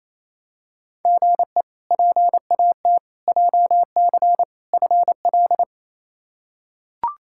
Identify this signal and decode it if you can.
Morse code, heard as a keyed tone, about 28 wpm, 700 Hz. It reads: GE PAT JC FL
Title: K1USN SST QSO Practice - 28wpm